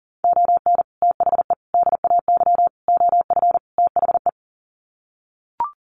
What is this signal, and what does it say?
Morse code, heard as a keyed tone, about 40 wpm, 700 Hz. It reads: ON THE DAY OF THE
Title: Sets of 5 Words Encore 40wpm